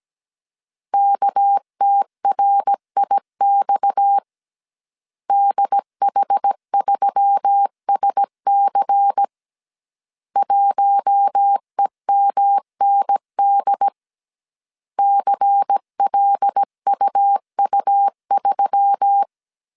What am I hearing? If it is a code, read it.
Morse code, heard as a keyed tone, about 17 wpm, 800 Hz. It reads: KTRIX DH3SC 1EMND CLUU3
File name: CW / DET / 1 6 1500C S A 16wpm 2635